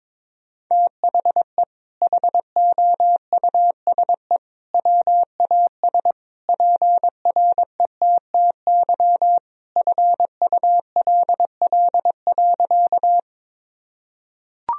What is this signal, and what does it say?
Morse code, heard as a keyed tone, about 22 wpm, 700 Hz. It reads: THE HOUSE WAS PRETTY FULL.
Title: Sentences from Top 1000 Words - Review Part 3 - 22wpm